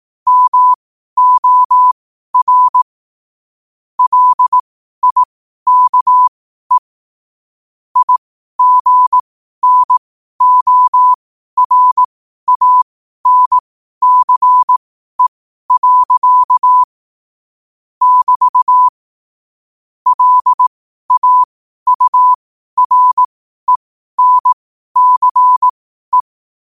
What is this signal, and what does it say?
Morse code, heard as a keyed tone, about 18 wpm, spaced as for 12 wpm, 1000 Hz. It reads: MOR LIKE IGNORANCE. <BT> LAURENCE
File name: qotdfiles / 12wpm